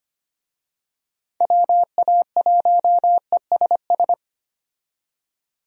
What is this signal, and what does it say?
Morse code, heard as a keyed tone, about 25 wpm, 700 Hz. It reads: WA1ESS
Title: USA Call Signs - 6 Characters - 25wpm